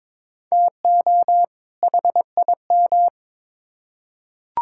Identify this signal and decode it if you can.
Morse code, heard as a keyed tone, about 21 wpm, 700 Hz. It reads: TO HIM
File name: Build A Phrase - 2-Word Phrases - 22wpm